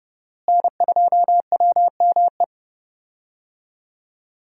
Morse code, heard as a keyed tone, about 32 wpm, 700 Hz. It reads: N2WME
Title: USA Call Signs - 5 Characters - Repeated 2x - 30wpm